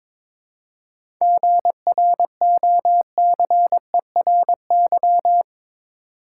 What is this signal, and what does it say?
Morse code, heard as a keyed tone, about 22 wpm, 700 Hz. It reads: GROCERY
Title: Seven Letter Words - 22wpm